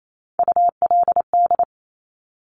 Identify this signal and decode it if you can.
Morse code, heard as a keyed tone, about 28 wpm, 700 Hz. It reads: ULD
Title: Top Three Letter Combinations in Words 28wpm